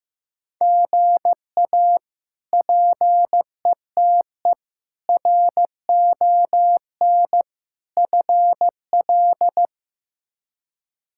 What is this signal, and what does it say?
Morse code, heard as a keyed tone, about 15 wpm, 700 Hz. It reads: GA PETE RON FL
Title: K1USN SST QSO Practice - 15wpm